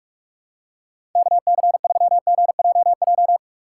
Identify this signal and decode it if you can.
Morse code, heard as a keyed tone, about 48 wpm, 700 Hz. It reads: KC3GJJ
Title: USA Call Signs - 6 Characters - 3x Character Spacing - 45wpm